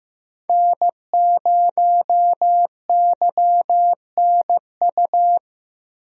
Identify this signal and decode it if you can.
Morse code, heard as a keyed tone, about 15 wpm, 700 Hz. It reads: N0YNU
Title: USA Call Signs - 5 Characters - Repeated 2x - 15wpm